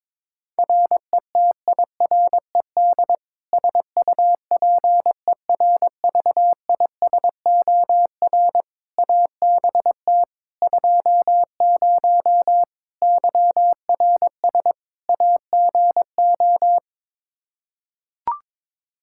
Morse code, heard as a keyed tone, about 22 wpm, 700 Hz. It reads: RETIRED SUPERVISOR ABT 20 YRS AGO
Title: Practice QSO - Exchange 3 - More Info - 22wpm